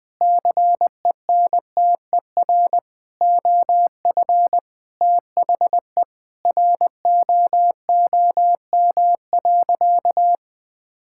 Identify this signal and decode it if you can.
Morse code, heard as a keyed tone, about 20 wpm, 700 Hz. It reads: CENTER OF THE ROOM.
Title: Sentences with 4-Word Phrases - Spoken Beginning - 20wpm